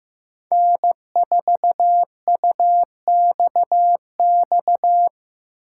Morse code, heard as a keyed tone, about 15 wpm, 700 Hz. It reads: N4UXX
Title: USA Call Signs - 5 Characters - 6x Character Spacing - 15wpm